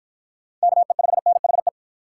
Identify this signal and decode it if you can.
Morse code, heard as a keyed tone, about 53 wpm, 700 Hz. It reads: KE5NHE